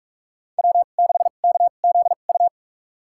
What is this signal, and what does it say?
Morse code, heard as a keyed tone, about 44 wpm, spaced as for 32 wpm, 700 Hz. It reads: W6KZU
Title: USA Call Signs - 5 Characters - 2x Character Spacing - 45wpm